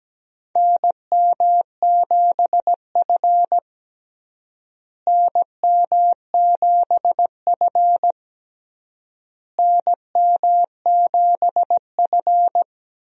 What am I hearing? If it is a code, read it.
Morse code, heard as a keyed tone, about 17 wpm, 700 Hz. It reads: NM7F NM7F NM7F
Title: USA Call Signs - 4 Characters - Repeated 3x - 17wpm